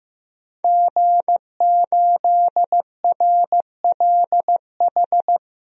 Morse code, heard as a keyed tone, about 15 wpm, 700 Hz. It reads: G8RLH
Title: DXCC-100 Call Signs - 5 Characters - 15wpm